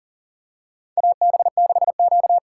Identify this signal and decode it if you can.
Morse code, heard as a keyed tone, about 44 wpm, 700 Hz. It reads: AB6Q
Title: USA Call Signs - 4 Characters - 6x Character Spacing - 40wpm